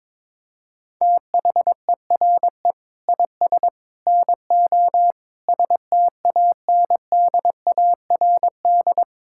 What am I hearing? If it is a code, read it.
Morse code, heard as a keyed tone, about 22 wpm, 700 Hz. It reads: THERE IS NO STANDARD
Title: Sentences from Top 1000 Words - Review Part 6 - 22wpm